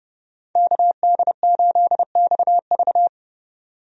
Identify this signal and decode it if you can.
Morse code, heard as a keyed tone, about 30 wpm, 700 Hz. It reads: KD8XV
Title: USA Call Signs - 5 Characters - 30wpm